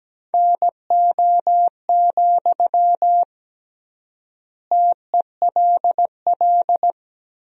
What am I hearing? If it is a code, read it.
Morse code, heard as a keyed tone, about 17 wpm, 700 Hz. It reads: NO, TELL
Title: 3x Word-Spacing - Sentences from Top 200 Words - 17wpm